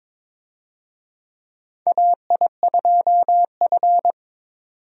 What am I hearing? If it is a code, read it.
Morse code, heard as a keyed tone, about 22 wpm, 700 Hz. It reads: AI2F
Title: USA Call Signs - 4 Characters - 22wpm